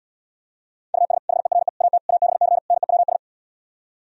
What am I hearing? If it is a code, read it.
Morse code, heard as a keyed tone, about 78 wpm, 700 Hz. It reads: US HERE AT ABC NEWS